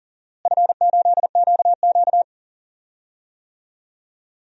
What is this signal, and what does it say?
Morse code, heard as a keyed tone, about 40 wpm, 700 Hz. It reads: F8QQ
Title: DXCC-100 Call Signs - 4 Characters - Repeated 2x - 40wpm